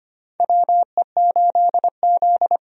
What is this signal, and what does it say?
Morse code, heard as a keyed tone, about 25 wpm, 700 Hz. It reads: WE8Z
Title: USA Call Signs - 4 Characters - 25wpm